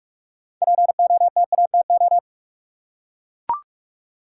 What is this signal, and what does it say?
Morse code, heard as a keyed tone, about 44 wpm, 700 Hz. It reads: POTATO
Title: Six Letter Words - 45wpm